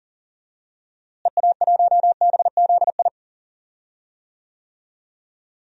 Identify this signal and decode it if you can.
Morse code, heard as a keyed tone, about 40 wpm, 700 Hz. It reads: EA1BZI
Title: DXCC-100 Call Signs - 6 Characters - Repeated 3x - 40wpm